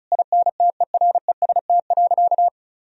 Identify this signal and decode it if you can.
Morse code, heard as a keyed tone, about 36 wpm, 700 Hz. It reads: INTEREST.
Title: Sentences from Top 1000 Words - Review Part 4 - 35wpm